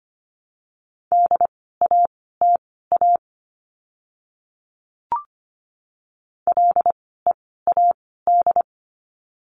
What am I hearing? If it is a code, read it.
Morse code, heard as a keyed tone, about 25 wpm, spaced as for 16 wpm, 700 Hz. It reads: DATA LEAD
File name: course / 25:10wpm / 094 Lesson 16f - Letter c - Words 25wpm